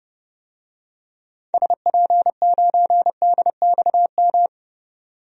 Morse code, heard as a keyed tone, about 31 wpm, 700 Hz. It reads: SP9DXM
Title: DXCC-100 Call Signs - 6 Characters - 30wpm